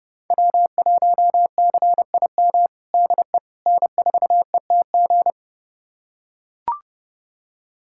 Morse code, heard as a keyed tone, about 30 wpm, 700 Hz. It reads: W1CIM DE N4ETG
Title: USA Callsign DE Callsign - 5 Characters - 30wpm